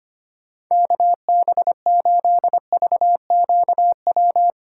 Morse code, heard as a keyed tone, about 25 wpm, 700 Hz. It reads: KB8VQW
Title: USA Call Signs - 2x Character Spacing - 25wpm